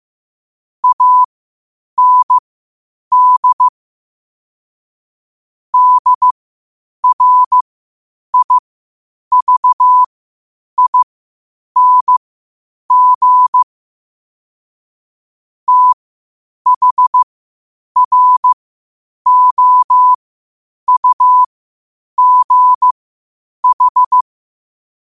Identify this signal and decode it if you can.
Morse code, heard as a keyed tone, about 15 wpm, spaced as for 8 wpm, 1000 Hz. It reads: AND DRIVING THROUGH
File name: qotdfiles / 8wpm